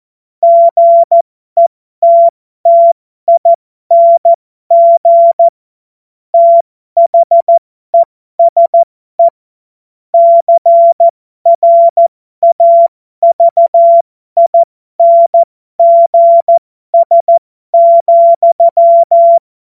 Morse code, heard as a keyed tone, about 14 wpm, spaced as for 12 wpm, 680 Hz. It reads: GETTING THESE CRAVINGS,
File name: Morsekurs / Witze